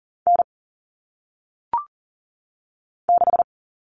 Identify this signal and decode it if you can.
Morse code, heard as a keyed tone, about 39 wpm, 700 Hz. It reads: N 6
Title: All Cut Numbers - 1-Digit 40wpm